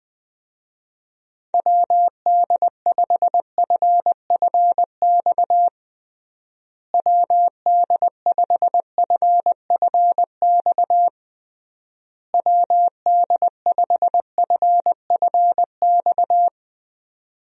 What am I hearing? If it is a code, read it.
Morse code, heard as a keyed tone, about 20 wpm, 700 Hz. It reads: WD5FFX WD5FFX WD5FFX
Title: USA Call Signs - 6 Characters - Repeated 3x - 20wpm